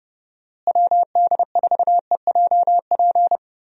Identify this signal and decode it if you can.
Morse code, heard as a keyed tone, about 30 wpm, 700 Hz. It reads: WD4EJP
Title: USA Call Signs - 6 Characters - Repeated 3x - 30wpm